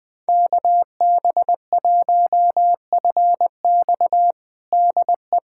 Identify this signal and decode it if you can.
Morse code, heard as a keyed tone, about 20 wpm, 700 Hz. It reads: KB1FX DE
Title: USA Callsign DE Callsign - 5 Characters - 20wpm